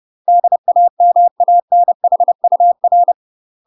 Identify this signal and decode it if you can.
Morse code, heard as a keyed tone, about 30 wpm, 700 Hz. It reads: DAMANHUR
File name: UK MASTER QSOs TRAINING / 30WPM / UK MASTER QSOs 0015